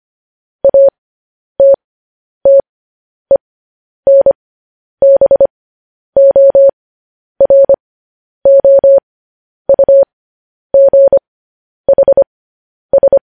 Morse code, heard as a keyed tone, about 25 wpm, spaced as for 10 wpm, 550 Hz. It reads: ATTENBOROUGHS
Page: news-headlines-in-morse-code-at-10-wpm